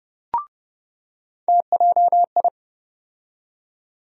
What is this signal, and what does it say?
Morse code, heard as a keyed tone, about 30 wpm, 700 Hz. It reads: TJI
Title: USA Call Sign Postfixes - 3 Letters 30wpm